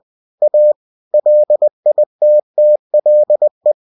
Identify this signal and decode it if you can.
Morse code, heard as a keyed tone, about 20 wpm, 600 Hz. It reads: A LITTLE
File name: cw-books / winnie-the-pooh / winnie-20wpm-page-0003